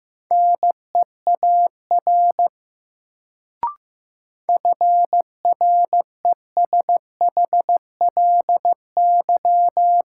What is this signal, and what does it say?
Morse code, heard as a keyed tone, about 15 wpm, 700 Hz. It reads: NEAR FRESHLY
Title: Sets of 5 Words Encore - New Word Drill - 15wpm